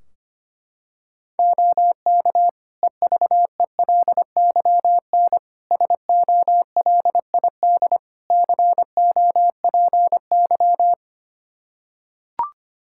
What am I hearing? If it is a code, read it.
Morse code, heard as a keyed tone, about 25 wpm, 700 Hz. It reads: OK EVELYN SOLID COPY
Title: Practice QSO - Exchange 3 - More Info - 25wpm